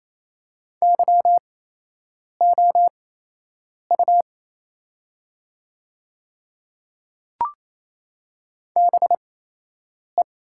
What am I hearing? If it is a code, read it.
Morse code, heard as a keyed tone, about 28 wpm, spaced as for 8 wpm, 700 Hz. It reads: YOU BE
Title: 8x Character Spacing - Top 100 Words - 28wpm